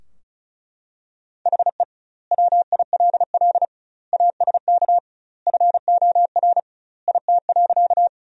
Morse code, heard as a keyed tone, about 36 wpm, 700 Hz. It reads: HE WILL ASK FOR IT.
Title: 2x Word-Spacing - Sentences from Top 200 Words - 35wpm